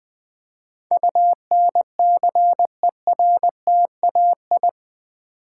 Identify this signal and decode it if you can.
Morse code, heard as a keyed tone, about 20 wpm, 700 Hz. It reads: UNCERTAI
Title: Build A Word - 4 Letters At A Time - 11-Letter Words - 20wpm